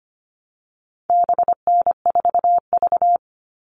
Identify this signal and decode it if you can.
Morse code, heard as a keyed tone, about 25 wpm, 700 Hz. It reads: BN4V
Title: All Cut Numbers - 4-Digit 25wpm